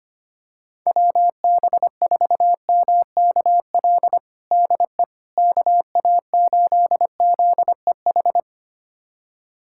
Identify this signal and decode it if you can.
Morse code, heard as a keyed tone, about 25 wpm, 700 Hz. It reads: WB4MKL DE KA8ZEH
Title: USA Callsign DE Callsign - 6 Characters - 25wpm